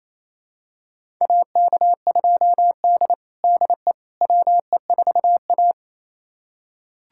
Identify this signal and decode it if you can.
Morse code, heard as a keyed tone, about 28 wpm, 700 Hz. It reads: AK2D DE WE4A
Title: USA Callsign DE Callsign - 4 Characters - 28wpm